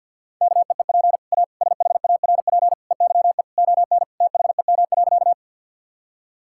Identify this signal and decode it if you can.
Morse code, heard as a keyed tone, about 50 wpm, 700 Hz. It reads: KEEP A SHARP EYE ON THEM.
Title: Sentences from Top 1600 Words 50wpm